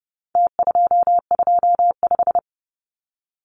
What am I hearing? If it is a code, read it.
Morse code, heard as a keyed tone, about 29 wpm, 700 Hz. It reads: T225
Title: Numbers - Contest Style - 4-Digit 30wpm